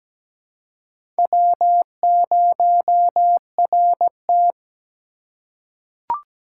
Morse code, heard as a keyed tone, about 17 wpm, 700 Hz. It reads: W0RT
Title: USA Call Signs - 4 Characters - 2x Character Spacing - 17wpm